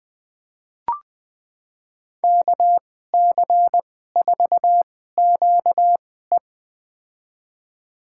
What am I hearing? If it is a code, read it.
Morse code, heard as a keyed tone, about 20 wpm, spaced as for 14 wpm, 700 Hz. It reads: KC4QE
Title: USA Call Signs - 5 Characters - 2x Character Spacing - 20wpm